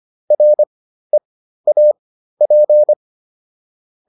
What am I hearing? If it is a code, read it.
Morse code, heard as a keyed tone, about 25 wpm, spaced as for 13 wpm, 600 Hz. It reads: REAP